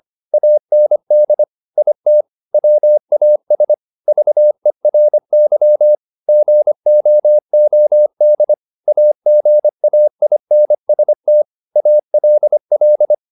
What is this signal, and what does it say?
Morse code, heard as a keyed tone, about 25 wpm, 600 Hz. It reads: AND IT WAS VERY GOOD AGAINST ALL
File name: the old man and the sea part 1 25wpm